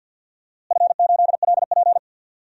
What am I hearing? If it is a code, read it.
Morse code, heard as a keyed tone, about 51 wpm, 700 Hz. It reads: F8LP
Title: DXCC-100 Call Signs - 4 Characters - Repeated 2x - 50wpm